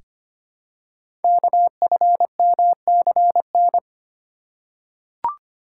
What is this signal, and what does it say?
Morse code, heard as a keyed tone, about 25 wpm, 700 Hz. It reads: KFMCN
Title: Code Groups - 5 Letters - 25wpm